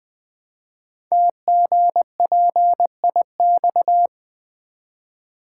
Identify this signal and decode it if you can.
Morse code, heard as a keyed tone, about 20 wpm, 700 Hz. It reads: TGPIX
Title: Code Groups - 5 Letters - 20wpm